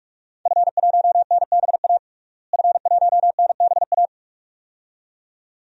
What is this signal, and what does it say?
Morse code, heard as a keyed tone, about 43 wpm, 700 Hz. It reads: F1NBA F1NBA
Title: DXCC-100 Call Signs - 5 Characters - Repeated 2x - 45wpm